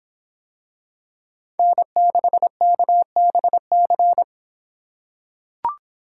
Morse code, heard as a keyed tone, about 25 wpm, 700 Hz. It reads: N6KBC